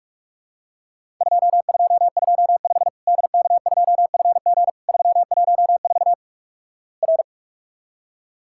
Morse code, heard as a keyed tone, about 46 wpm, 700 Hz. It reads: 2225 DK2FG 314 R
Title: Worked All Europe DX Contest - QTC Exchange - 45wpm